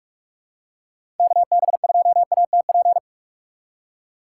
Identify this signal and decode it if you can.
Morse code, heard as a keyed tone, about 42 wpm, 700 Hz. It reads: KB2ATP
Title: USA Call Signs - 6 Characters - 6x Character Spacing - 45wpm